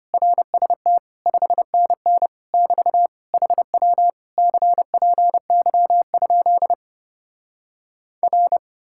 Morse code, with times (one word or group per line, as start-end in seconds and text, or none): 0.1-1.0 RST
1.3-2.3 5NN
2.5-3.1 <BT>
3.3-4.1 HW
4.4-6.7 CPY?
8.2-8.6 R